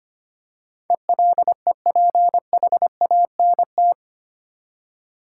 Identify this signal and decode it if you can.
Morse code, heard as a keyed tone, about 25 wpm, 700 Hz. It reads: ELEPHANT
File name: practice / 25wpm / Easy Words with Hints - 25wpm